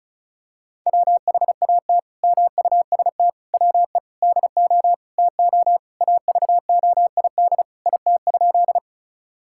Morse code, heard as a keyed tone, about 34 wpm, 700 Hz. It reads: WHAT MUST WE DO TO AVOID IT?
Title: Sentences from Top 1000 Words - Review Part 6 - 35wpm